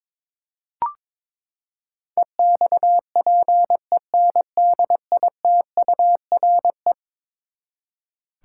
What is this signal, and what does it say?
Morse code, heard as a keyed tone, about 22 wpm, 700 Hz. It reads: EXPENDITURE
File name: practice / 22wpm / Eleven Letter Words - 22wpm